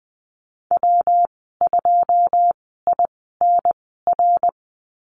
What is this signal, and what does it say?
Morse code, heard as a keyed tone, about 19 wpm, spaced as for 14 wpm, 700 Hz. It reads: W2INR
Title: 122 Lesson 20g - Number 3 - Call Signs 20wpm